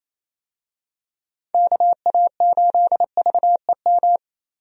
Morse code, heard as a keyed tone, about 28 wpm, 700 Hz. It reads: KA8VEM
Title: USA Call Signs - 6 Characters - Repeated 2x - 28wpm